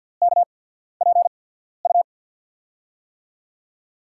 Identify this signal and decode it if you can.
Morse code, heard as a keyed tone, about 47 wpm, spaced as for 14 wpm, 700 Hz. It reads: KPU
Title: USA Call Signs - 6 Characters - 8x Character Spacing - 50wpm